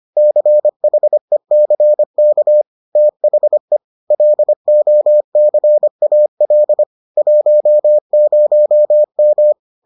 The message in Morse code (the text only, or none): CHECK THE LOCAL 10M